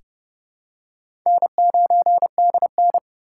Morse code, heard as a keyed tone, about 29 wpm, 700 Hz. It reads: N9DN
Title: USA Call Signs - 4 Characters - 2x Character Spacing - 30wpm